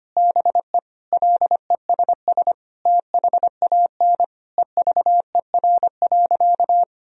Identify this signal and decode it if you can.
Morse code, heard as a keyed tone, about 25 wpm, 700 Hz. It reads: BE LESS THAN EVER.
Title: Sentences from Top 1000 Words - Review Part 2 - 25wpm